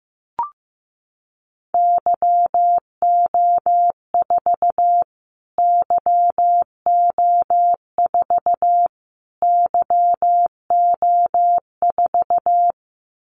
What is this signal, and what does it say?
Morse code, heard as a keyed tone, about 15 wpm, 700 Hz. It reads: YO4 YO4 YO4
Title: Callsign Prefixes for Top 25 DXCC - Repeated 3x - 15wpm